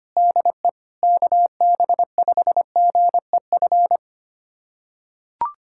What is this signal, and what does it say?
Morse code, heard as a keyed tone, about 25 wpm, 700 Hz. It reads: DE KB5GEF
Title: USA Callsign DE Callsign - 6 Characters - 25wpm